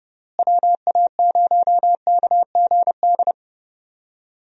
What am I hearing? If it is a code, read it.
Morse code, heard as a keyed tone, about 30 wpm, 700 Hz. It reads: WA0KGD
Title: USA Call Signs - 6 Characters - Repeated 2x - 30wpm